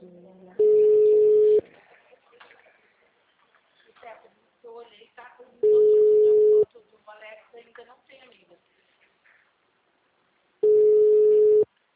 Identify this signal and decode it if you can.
Ringback tone (Europe, 425 Hz), with speech.
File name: out- -2008